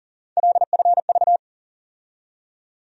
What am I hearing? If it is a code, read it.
Morse code, heard as a keyed tone, about 40 wpm, 700 Hz. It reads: LFV
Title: USA Call Sign Postfixes - 3 Letters 40wpm